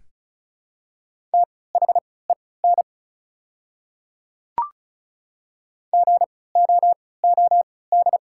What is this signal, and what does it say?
Morse code, heard as a keyed tone, about 36 wpm, spaced as for 20 wpm, 700 Hz. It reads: THEN GOOD
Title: 3x Character Spacing - Top 200 Words - 35wpm